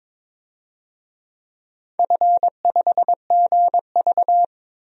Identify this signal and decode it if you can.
Morse code, heard as a keyed tone, about 22 wpm, 700 Hz. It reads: F5GV
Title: DXCC-100 Call Signs - 4 Characters - 22wpm